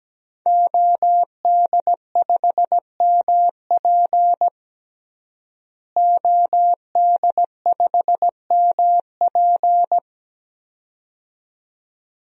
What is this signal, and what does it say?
Morse code, heard as a keyed tone, about 17 wpm, 700 Hz. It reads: OD5MP OD5MP
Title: DXCC-100 Call Signs - 5 Characters - Repeated 2x - 17wpm